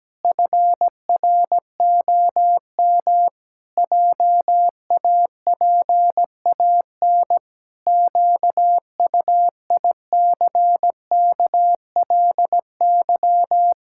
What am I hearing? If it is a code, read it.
Morse code, heard as a keyed tone, about 17 wpm, 700 Hz. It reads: FROM JAPAN QUICKLY
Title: Pangram Challenge v2 - 17wpm